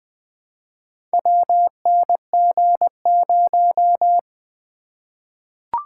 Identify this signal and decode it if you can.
Morse code, heard as a keyed tone, about 21 wpm, 700 Hz. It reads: WNG0
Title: Four Letter-Number Combinations - 20wpm